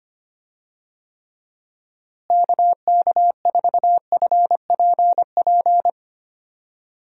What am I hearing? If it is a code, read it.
Morse code, heard as a keyed tone, about 25 wpm, 700 Hz. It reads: KK4FPP
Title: USA Call Signs - 6 Characters - Repeated 3x - 25wpm